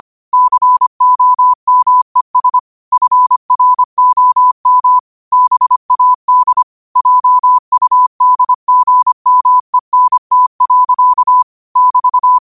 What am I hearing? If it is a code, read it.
Morse code, heard as a keyed tone, about 25 wpm, 1000 Hz. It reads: COMES FROM BAD JUDGMENT. <BT>
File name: qotdfiles / 25wpm